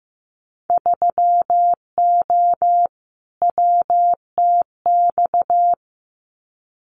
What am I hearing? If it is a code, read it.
Morse code, heard as a keyed tone, about 15 wpm, 700 Hz. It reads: 3O WTX
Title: Winter Field Day Exchanges Encore 15wpm